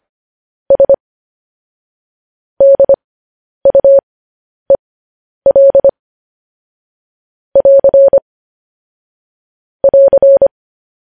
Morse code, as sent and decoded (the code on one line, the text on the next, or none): ... / -.. ..- . .-.. / .-.-. / .-.-.
S DUEL <AR> <AR>